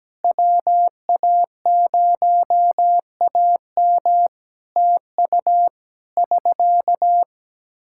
Morse code, heard as a keyed tone, about 17 wpm, 700 Hz. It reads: WA0AM TU <SK>
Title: Practice QSO - Exchange 4 - Ending - 17wpm